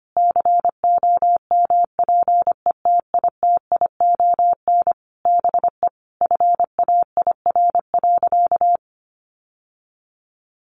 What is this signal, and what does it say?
Morse code, heard as a keyed tone, about 25 wpm, 700 Hz. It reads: COMPETITION BE FAIR.
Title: Sentences from Top 1300 Words 25wpm